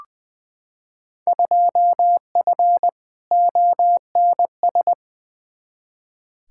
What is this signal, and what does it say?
Morse code, heard as a keyed tone, about 20 wpm, 700 Hz. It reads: 2F ONS
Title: Field Day Exchanges - 20wpm